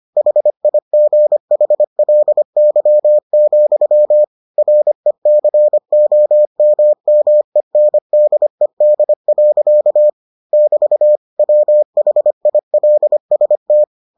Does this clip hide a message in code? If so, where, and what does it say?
Yes: Morse code from 0.2 to 13.8 s, heard as a keyed tone, about 25 wpm, 600 Hz. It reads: HIGHLY, RECOMMENDED. <BT> WHILST